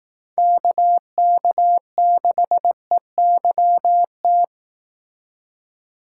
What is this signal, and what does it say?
Morse code, heard as a keyed tone, about 18 wpm, 700 Hz. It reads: KK6EYT
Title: Speed Racing - USA 2x3 Call Signs 15wpm